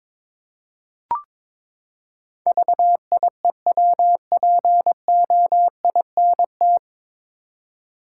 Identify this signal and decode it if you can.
Morse code, heard as a keyed tone, about 22 wpm, 700 Hz. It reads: VIEWPOINT
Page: Morse Code Ninja practice 22wpm